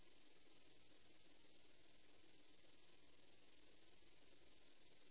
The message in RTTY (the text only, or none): none